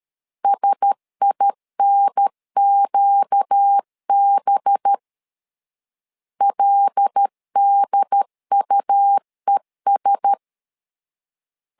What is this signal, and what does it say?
Morse code, heard as a keyed tone, about 13 wpm, 800 Hz. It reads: SINQB LDUES
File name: LEZIONI / 5 LBVC / 5 LBVC 12wpm 1955